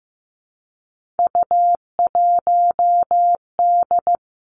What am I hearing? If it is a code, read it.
Morse code, heard as a keyed tone, about 15 wpm, 700 Hz. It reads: U1D
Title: Three Challenging Letter-Number Combinations 15wpm